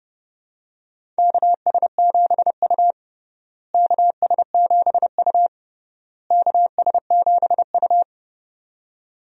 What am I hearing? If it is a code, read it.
Morse code, heard as a keyed tone, about 29 wpm, 700 Hz. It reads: KS7U KS7U KS7U
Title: USA Call Signs - 4 Characters - Repeated 3x - 30wpm